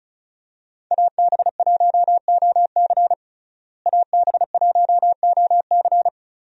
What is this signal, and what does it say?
Morse code, heard as a keyed tone, about 36 wpm, 700 Hz. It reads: AB1OC AB1OC
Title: USA Call Signs - 5 Characters - Repeated 3x - 35wpm